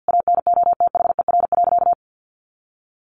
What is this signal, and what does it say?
Morse code, heard as a keyed tone, about 50 wpm, 700 Hz. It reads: ANOTHER.
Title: Sentences from Top 1400 Words 50wpm